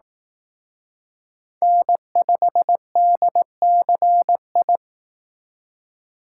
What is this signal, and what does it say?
Morse code, heard as a keyed tone, about 18 wpm, 700 Hz. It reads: N5DCI
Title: Speed Racing - USA 1x3 Call Signs 15wpm